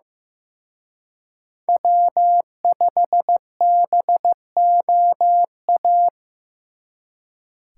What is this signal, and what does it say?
Morse code, heard as a keyed tone, about 18 wpm, 700 Hz. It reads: W5BOA
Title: Speed Racing - USA 1x3 Call Signs 15wpm